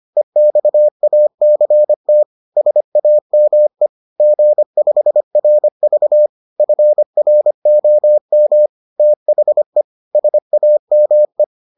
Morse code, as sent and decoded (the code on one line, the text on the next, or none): . -..- .- -.-. - / ... .- -- . / --. ..... .-. ...- / ..-. .-. --- -- / - .... / ... .- -- .
EXACT SAME G5RV FROM THE SAME